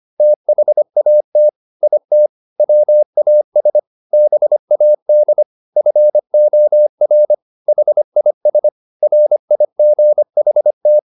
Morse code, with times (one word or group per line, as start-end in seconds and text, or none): 0.2-1.5 THAT
1.8-2.3 IT
2.6-3.8 WAS
4.1-5.4 BAD
5.8-7.3 FOR
7.7-8.7 HIS
9.0-11.0 RIGHT